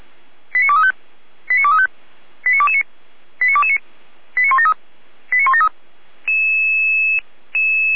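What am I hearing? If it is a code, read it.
Five-tone selective call (ZVEI), heard as quick runs of five tones, tones of about 70 ms. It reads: 89235 89235 89229 89229 89173 89173
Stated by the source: ZVEI five-tone call received by SDR, from the feuerwehr-orlamuende.org fire brigade audio archive.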